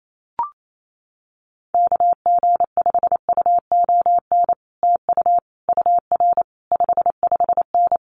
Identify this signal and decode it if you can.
Morse code, heard as a keyed tone, about 28 wpm, 700 Hz. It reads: KG5UON TU UR 55N
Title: Common QSO Phrases Encore 28wpm